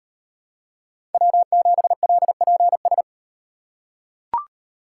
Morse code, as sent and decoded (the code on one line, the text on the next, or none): .-- --... .-.. .--. ...
W7LPS